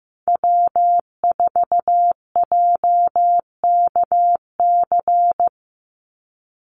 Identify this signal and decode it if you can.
Morse code, heard as a keyed tone, about 15 wpm, 700 Hz. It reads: W4JKC
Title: NA SOTA Chaser Callsigns 15wpm